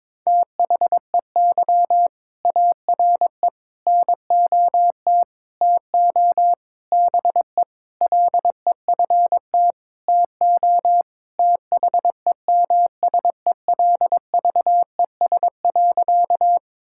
Morse code, heard as a keyed tone, about 22 wpm, 700 Hz. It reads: THEY ARE NOT TO BE LEFT TO THEMSELVES.
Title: Sentences from Top 1000 Words - Review Part 5 - 22wpm